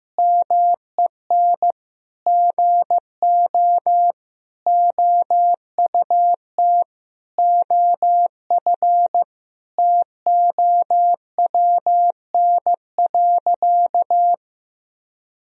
Morse code, heard as a keyed tone, about 15 wpm, 700 Hz. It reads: MEN GO OUT OF TOWN.
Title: Sentences from Top 1000 Words - Review Part 4 - 15wpm